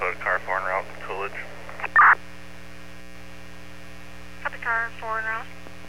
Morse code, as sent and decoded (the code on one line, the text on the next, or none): none